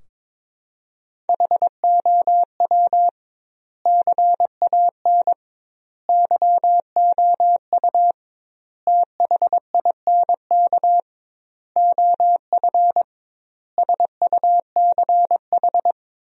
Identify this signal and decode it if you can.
Morse code, heard as a keyed tone, about 22 wpm, 700 Hz. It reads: HOW CAN YOU THINK OF SUCH
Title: 2x Word-Spacing - Sentences from Top 200 Words - 22wpm